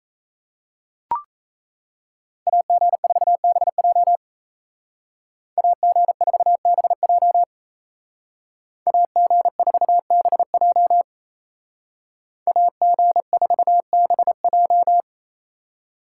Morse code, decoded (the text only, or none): AG4BJ AG4BJ AG4BJ AG4BJ